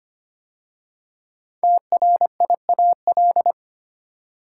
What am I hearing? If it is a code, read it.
Morse code, heard as a keyed tone, about 25 wpm, 700 Hz. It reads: TRIAL
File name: 5 Letter Words - First 3 Letters Given - 25wpm